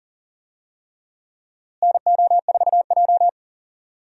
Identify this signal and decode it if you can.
Morse code, heard as a keyed tone, about 40 wpm, 700 Hz. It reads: NO4J
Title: USA Call Signs - 4 Characters - Repeated 3x - 40wpm